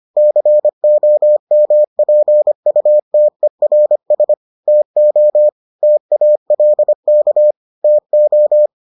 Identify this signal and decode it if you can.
Morse code, heard as a keyed tone, about 25 wpm, 600 Hz. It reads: COMPUTERS TO TALK TO